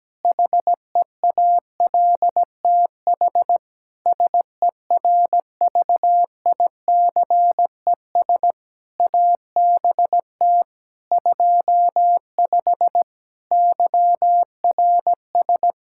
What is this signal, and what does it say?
Morse code, heard as a keyed tone, about 17 wpm, 700 Hz. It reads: HEALTH SERVICES ABT 25 YRS
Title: Practice QSO - Exchange 3 - More Info - 17wpm